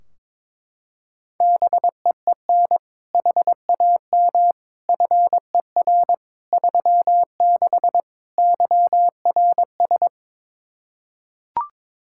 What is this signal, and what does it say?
Morse code, heard as a keyed tone, about 22 wpm, 700 Hz. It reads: BEEN HAM FER 36 YRS
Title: Practice QSO - Exchange 3 - More Info - 22wpm